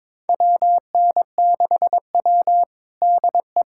Morse code, heard as a keyed tone, about 22 wpm, 700 Hz. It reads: WN6W DE
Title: USA Callsign DE Callsign - 4 Characters - 22wpm